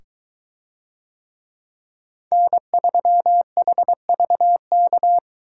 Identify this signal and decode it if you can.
Morse code, heard as a keyed tone, about 21 wpm, 700 Hz. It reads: N3HVK